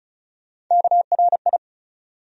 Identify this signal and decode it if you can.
Morse code, heard as a keyed tone, about 36 wpm, 700 Hz. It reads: KRI
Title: USA Call Sign Postfixes - 3 Letters 35wpm